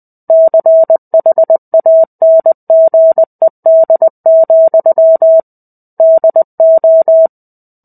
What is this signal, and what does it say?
Morse code, heard as a keyed tone, about 20 wpm, 650 Hz. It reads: CHANGED, DO
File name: 05-Chapter-05 20 WPM